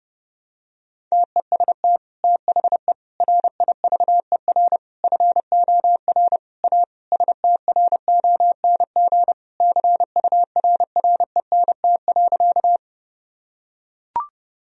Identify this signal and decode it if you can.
Morse code, heard as a keyed tone, about 30 wpm, 700 Hz. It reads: TEST THE RIVER FOR A STRONG CURRENT.
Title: Sentences from Top 1000 Words - Review Part 4 - 30wpm